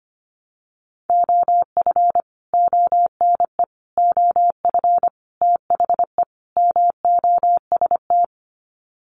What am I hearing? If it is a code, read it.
Morse code, heard as a keyed tone, about 25 wpm, 700 Hz. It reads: OF ONE OF THE MOST
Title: Sets of 5 Words Encore2 25wpm